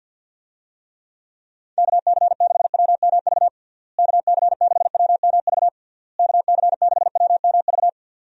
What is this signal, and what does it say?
Morse code, heard as a keyed tone, about 48 wpm, 700 Hz. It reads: KC6WMV KC6WMV KC6WMV
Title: USA Call Signs - 6 Characters - Repeated 3x - 50wpm